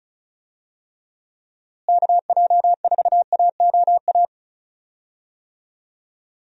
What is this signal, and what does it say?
Morse code, heard as a keyed tone, about 36 wpm, 700 Hz. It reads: KJ4AOA
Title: USA Call Signs - 6 Characters - Repeated 2x - 35wpm